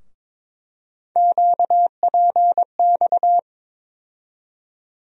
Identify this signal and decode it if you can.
Morse code, heard as a keyed tone, about 22 wpm, 700 Hz. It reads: QPX
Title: USA Call Sign Postfixes - 3 Letters 22wpm